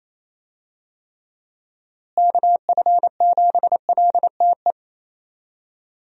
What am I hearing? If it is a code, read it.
Morse code, heard as a keyed tone, about 28 wpm, 700 Hz. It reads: KF7LTE
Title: USA Call Signs - 6 Characters - Repeated 2x - 28wpm